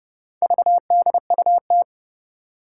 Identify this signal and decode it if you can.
Morse code, heard as a keyed tone, about 30 wpm, 700 Hz. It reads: VDUT
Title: Four Challenging Letter-Number Combinations 30wpm